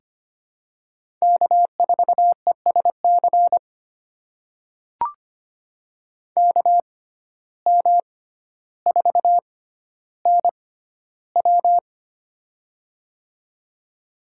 Morse code, heard as a keyed tone, about 25 wpm, spaced as for 13 wpm, 700 Hz. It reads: K4ESC KM4NW
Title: USA Call Signs - 5 Characters - 6x Character Spacing - 25wpm